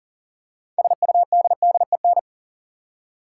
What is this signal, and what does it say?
Morse code, heard as a keyed tone, about 37 wpm, 700 Hz. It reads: SUDDEN
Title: Six Letter Words - 40wpm